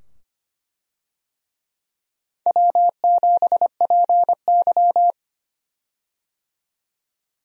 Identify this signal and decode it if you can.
Morse code, heard as a keyed tone, about 25 wpm, 700 Hz. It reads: W7PY
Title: USA Call Signs - 4 Characters - 25wpm